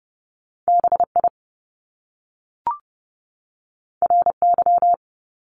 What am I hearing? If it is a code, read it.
Morse code, heard as a keyed tone, about 30 wpm, 700 Hz. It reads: BI RY
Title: Top Two Letter Combinations in Words 30wpm